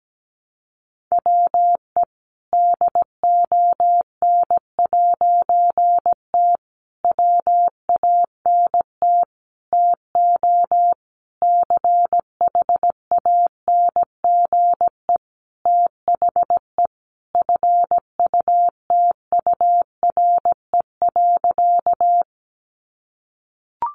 Morse code, heard as a keyed tone, about 17 wpm, 700 Hz. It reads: WE DON'T WANT TO CHANGE THE FUTURE.
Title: A Sound Of Thunder 17wpm